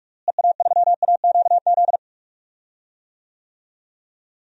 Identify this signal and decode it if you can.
Morse code, heard as a keyed tone, about 41 wpm, 700 Hz. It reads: EA3AQZ
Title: DXCC-100 Call Signs - 6 Characters - Repeated 2x - 45wpm